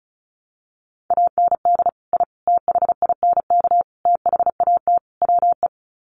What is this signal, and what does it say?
Morse code, heard as a keyed tone, about 34 wpm, 700 Hz. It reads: AND I THINK THAT WE
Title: Sets of 5 Words Encore2 35wpm